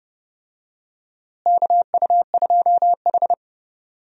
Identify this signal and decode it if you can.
Morse code, heard as a keyed tone, about 30 wpm, 700 Hz. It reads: KU2H
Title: USA Call Signs - 4 Characters - Repeated 3x - 30wpm